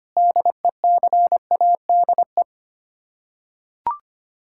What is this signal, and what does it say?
Morse code, heard as a keyed tone, about 25 wpm, 700 Hz. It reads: DECADE
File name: Sets of 9 Words - 25wpm